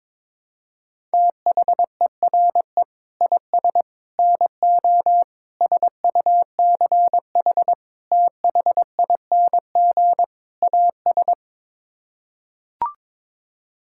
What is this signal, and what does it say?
Morse code, heard as a keyed tone, about 22 wpm, 700 Hz. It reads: THERE IS NO SUCH THING AS
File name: practice / 22wpm / Sets of 6 Words - 22wpm